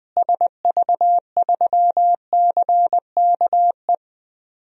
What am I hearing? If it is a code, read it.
Morse code, heard as a keyed tone, about 20 wpm, 700 Hz. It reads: SV3CKE
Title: DXCC-100 Call Signs - 6 Characters - Repeated 3x - 20wpm